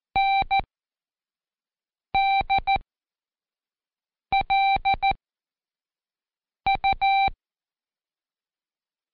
Morse code, heard as a keyed tone, about 14 wpm, spaced as for 5 wpm, 780 Hz. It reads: NDLU
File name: Code Quick / AudioDownld1 / Seg37-Day4-Quiz CHLN